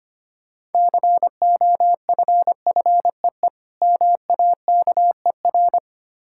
Morse code, heard as a keyed tone, about 25 wpm, 700 Hz. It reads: COFFEE MAKER
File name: Easy Words - Speed-Racing with Hints - 25wpm